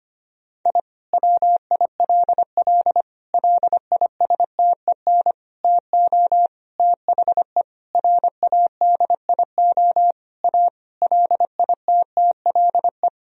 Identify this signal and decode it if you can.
Morse code, heard as a keyed tone, about 25 wpm, 700 Hz. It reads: I WILL LISTEN TO THE RADIO A LITTLE
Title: Sentences from Top 1000 Words - Review Part 3 - 25wpm